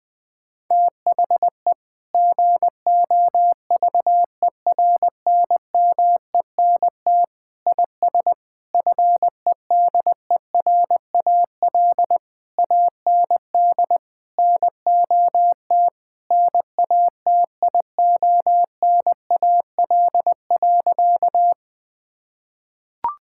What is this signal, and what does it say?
Morse code, heard as a keyed tone, about 20 wpm, 700 Hz. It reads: THE GOVERNMENT IS FEDERAL AND NOT NATIONAL.
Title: Sentences from Top 1000 Words - Review Part 5 - 20wpm